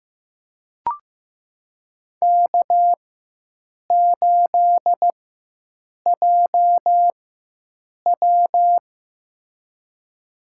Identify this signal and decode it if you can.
Morse code, heard as a keyed tone, about 15 wpm, spaced as for 7 wpm, 700 Hz. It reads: K8JW
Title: USA Call Signs - 4 Characters - 4x Character Spacing - 15wpm